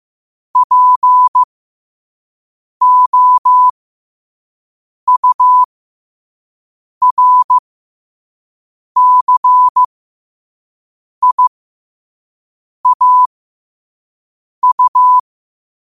Morse code, heard as a keyed tone, about 15 wpm, spaced as for 5 wpm, 1000 Hz. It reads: POURCIAU
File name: qotdfiles / 5wpm